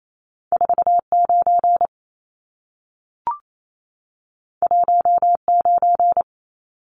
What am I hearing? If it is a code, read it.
Morse code, heard as a keyed tone, about 28 wpm, 700 Hz. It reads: 49 19
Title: Numbers - 2-Digit 28wpm